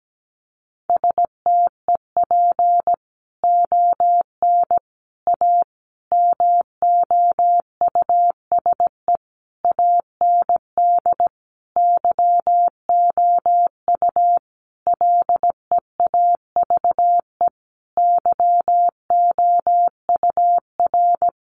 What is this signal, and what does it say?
Morse code, heard as a keyed tone, about 17 wpm, 700 Hz. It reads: STEP ON A MOUSE AND YOU LEAVE YOUR
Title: A Sound Of Thunder 17wpm